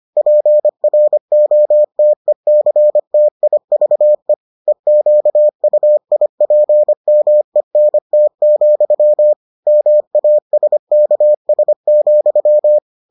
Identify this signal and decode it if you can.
Morse code, heard as a keyed tone, about 25 wpm, 600 Hz. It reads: PROTECTIVE EQUIPMENT, MASKS,